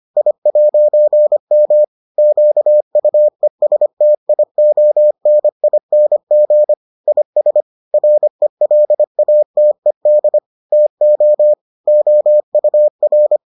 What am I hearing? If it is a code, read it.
Morse code, heard as a keyed tone, about 25 wpm, 600 Hz. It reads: I'M QUESTIONING IS RELATED TO OUR